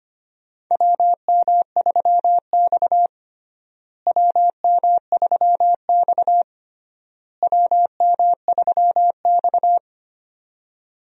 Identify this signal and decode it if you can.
Morse code, heard as a keyed tone, about 25 wpm, 700 Hz. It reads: WM3X WM3X WM3X
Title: USA Call Signs - 4 Characters - Repeated 3x - 25wpm